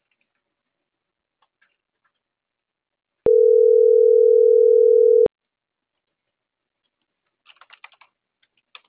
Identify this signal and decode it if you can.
Ringback tone (North America), 440 + 480 Hz.